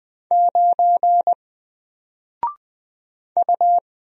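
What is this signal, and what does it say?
Morse code, heard as a keyed tone, about 20 wpm, 700 Hz. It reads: 9 U
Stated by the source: Single letter and number Morse drill at 20 wpm, from N4LCD.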